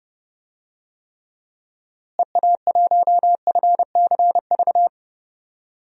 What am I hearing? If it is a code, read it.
Morse code, heard as a keyed tone, about 30 wpm, 700 Hz. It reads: EA1FCV
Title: DXCC-100 Call Signs - 6 Characters - Repeated 2x - 30wpm